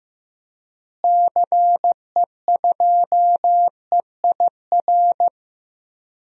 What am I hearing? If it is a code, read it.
Morse code, heard as a keyed tone, about 15 wpm, 700 Hz. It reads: CE2EIR
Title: DXCC-100 Call Signs - 6 Characters - Repeated 3x - 15wpm